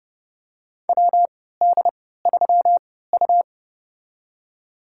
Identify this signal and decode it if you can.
Morse code, heard as a keyed tone, about 30 wpm, spaced as for 17 wpm, 700 Hz. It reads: WD3U
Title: USA Call Signs - 4 Characters - 3x Character Spacing - 30wpm